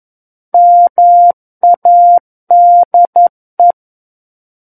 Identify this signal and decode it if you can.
Morse code, heard as a keyed tone, about 11 wpm, 700 Hz. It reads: MADE
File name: Book One / 02-Chapter 2 - 11 WPM